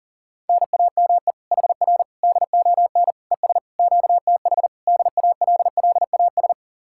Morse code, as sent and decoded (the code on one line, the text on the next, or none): -. .- -- . / .... .-. / -.. --- -. / . ... / --.- - .... / -.. .- .-.. .-.. .- ...
NAME HR DON ES QTH DALLAS